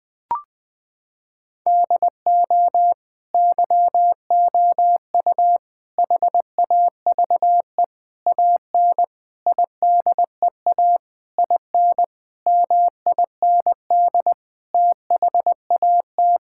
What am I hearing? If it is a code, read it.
Morse code, heard as a keyed tone, about 20 wpm, 700 Hz. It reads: DO YOU HAVE AN IDEA IN MIND THAT
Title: Sentences with 9-Word Phrases - 20wpm